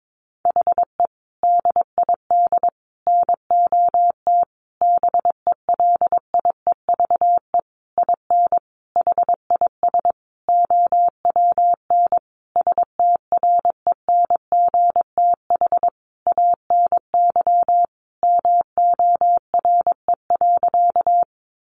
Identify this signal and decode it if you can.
Morse code, heard as a keyed tone, about 22 wpm, 700 Hz. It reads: HE DID NOT BELIEVE IN HIS OWN STRENGTH ANY MORE.
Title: Sentences from Top 1300 Words 22wpm